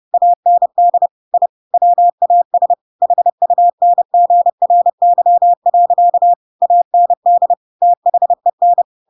Morse code, heard as a keyed tone, about 30 wpm, 700 Hz. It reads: AND I WAS HUNGRY. AND THEN
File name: Pinocchio / Pinocchio ch7 30 30 ik1hge